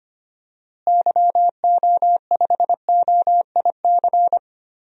Morse code, heard as a keyed tone, about 25 wpm, 700 Hz. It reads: YO5OIC
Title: DXCC-100 Call Signs - 6 Characters - Repeated 2x - 25wpm